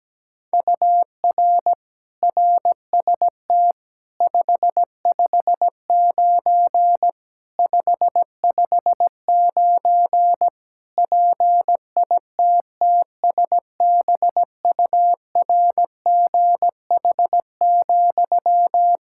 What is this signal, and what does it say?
Morse code, heard as a keyed tone, about 17 wpm, 700 Hz. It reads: UR RST 559 559 PITTSBURGH,
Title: Practice QSO - Exchange 1 - Beginning - 17wpm